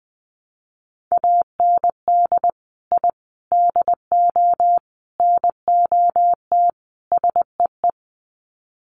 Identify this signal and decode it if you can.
Morse code, heard as a keyed tone, about 20 wpm, 700 Hz. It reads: AND I DO NOT SEE
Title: Sets of 5 Words Encore2 20wpm